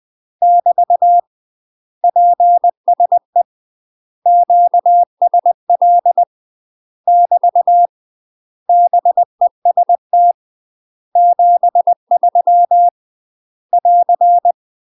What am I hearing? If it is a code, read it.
Morse code, heard as a keyed tone, about 20 wpm, 700 Hz. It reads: <BT> PSE QSL <BT> BEST 73 <AR>
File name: UK MASTER QSOs TRAINING / 20WPM / UK MASTER QSOs 0066